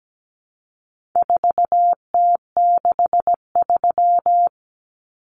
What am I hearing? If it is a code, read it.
Morse code, heard as a keyed tone, about 17 wpm, 700 Hz. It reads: 4T63
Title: Common Cut Numbers - 4-Digit 17wpm